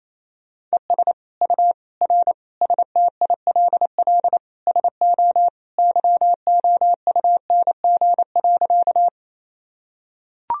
Morse code, heard as a keyed tone, about 28 wpm, 700 Hz. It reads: ES U R STILL SO YOUNG.
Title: Sentences from Top 400 Words and Abbreviations 28wpm